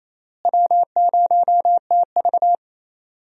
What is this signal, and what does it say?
Morse code, heard as a keyed tone, about 27 wpm, 700 Hz. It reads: W0TV
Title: USA Call Signs - 4 Characters - Repeated 3x - 28wpm